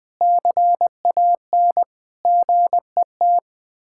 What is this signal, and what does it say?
Morse code, heard as a keyed tone, about 20 wpm, 700 Hz. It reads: CAN GET
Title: Sets of 8 Words - 20wpm